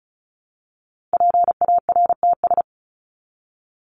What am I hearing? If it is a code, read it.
Morse code, heard as a keyed tone, about 34 wpm, 700 Hz. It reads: PARTS
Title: ES:S Suffix - Top 100 Words with Base Words 35wpm